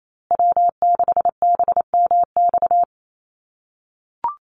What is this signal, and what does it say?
Morse code, heard as a keyed tone, about 28 wpm, 700 Hz. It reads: W6BMX
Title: USA Call Signs Repeated 3x 28wpm